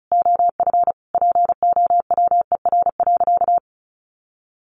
Morse code, heard as a keyed tone, about 34 wpm, 700 Hz. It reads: OF POWER.
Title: Sentences from Top 1200 Words 35wpm